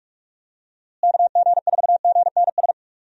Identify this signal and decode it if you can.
Morse code, heard as a keyed tone, about 41 wpm, 700 Hz. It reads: KG4GNS
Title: USA Call Signs - 6 Characters - Repeated 2x - 45wpm